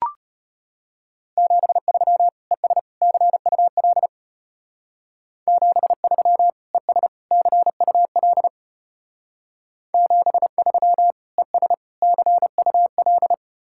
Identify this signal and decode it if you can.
Morse code, heard as a keyed tone, about 32 wpm, 700 Hz. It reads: 73 ES CUL 73 ES CUL 73 ES CUL